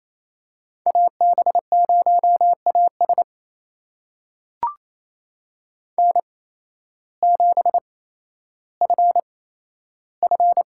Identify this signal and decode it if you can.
Morse code, heard as a keyed tone, about 28 wpm, spaced as for 13 wpm, 700 Hz. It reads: AB0AS N7FF
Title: USA Call Signs - 5 Characters - 8x Character Spacing - 28wpm